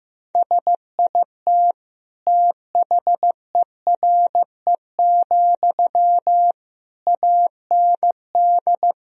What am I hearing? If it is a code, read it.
Morse code, heard as a keyed tone, about 14 wpm, 700 Hz. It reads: SIT THERE, AND
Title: Sentences from Top 1000 Words - Review Part 1 - 15wpm